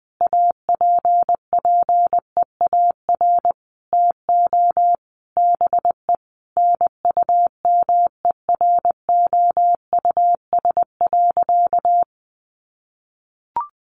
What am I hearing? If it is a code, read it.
Morse code, heard as a keyed tone, about 20 wpm, 700 Hz. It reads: APPEAR TO BE NUMEROUS.
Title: Sentences from Top 1400 Words 20wpm